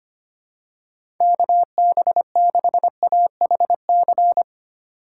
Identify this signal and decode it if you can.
Morse code, heard as a keyed tone, about 25 wpm, 700 Hz. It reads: KB6AHC
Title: USA Call Signs - 6 Characters - 8x Character Spacing - 25wpm